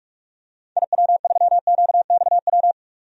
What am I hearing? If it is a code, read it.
Morse code, heard as a keyed tone, about 46 wpm, 700 Hz. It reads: IW3QXW
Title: DXCC-100 Call Signs - 6 Characters - 45wpm